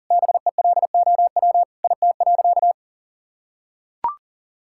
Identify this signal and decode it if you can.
Morse code, heard as a keyed tone, about 40 wpm, 700 Hz. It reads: BELOW IT.
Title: Sentences from Top 1000 Words - Review Part 5 - 40wpm